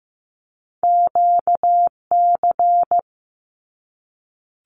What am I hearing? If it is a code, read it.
Morse code, heard as a keyed tone, about 15 wpm, 700 Hz. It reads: QC
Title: Canadian Provinces and Territories 15wpm